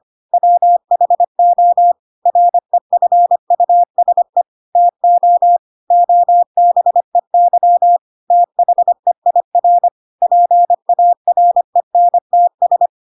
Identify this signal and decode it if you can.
Morse code, heard as a keyed tone, about 25 wpm, 700 Hz. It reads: WHO REFUSE TO OBEY THEIR PARENTS
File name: Pinocchio / Pinocchio ch4 25 25 ik1hge